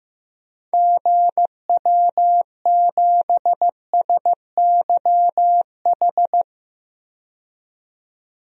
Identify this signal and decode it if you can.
Morse code, heard as a keyed tone, about 15 wpm, 700 Hz. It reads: GW7SYH
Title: DXCC-100 Call Signs - 6 Characters - 15wpm